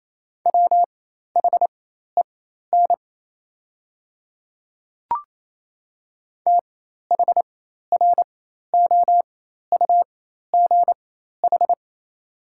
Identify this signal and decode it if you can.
Morse code, heard as a keyed tone, about 28 wpm, spaced as for 13 wpm, 700 Hz. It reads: WHEN THROUGH
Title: 4x Character Spacing - Top 200 Words - 28wpm